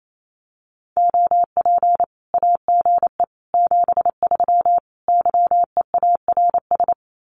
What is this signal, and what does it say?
Morse code, heard as a keyed tone, about 28 wpm, 700 Hz. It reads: OP AGE 73 YEARS
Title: Common QSO Phrases 28wpm